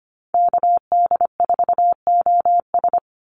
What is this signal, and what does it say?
Morse code, heard as a keyed tone, about 25 wpm, 700 Hz. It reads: KD4OS
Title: Call Signs - 25wpm